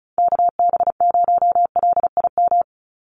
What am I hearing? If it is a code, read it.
Morse code, heard as a keyed tone, about 34 wpm, 700 Hz. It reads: KB0LIM
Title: USA Call Signs 35wpm